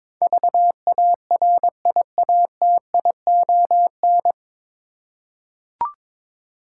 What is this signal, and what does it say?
Morse code, heard as a keyed tone, about 22 wpm, 700 Hz. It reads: VARIATION
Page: Morse Code Ninja practice 22wpm